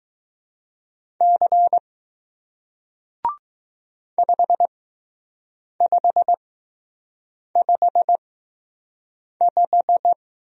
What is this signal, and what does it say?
Morse code, heard as a keyed tone, about 20 wpm, spaced as for 6 wpm, 700 Hz. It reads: C 5555